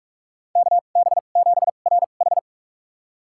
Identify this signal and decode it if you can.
Morse code, heard as a keyed tone, about 44 wpm, spaced as for 32 wpm, 700 Hz. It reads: KB7RH
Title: USA Call Signs - 2x Character Spacing - 45wpm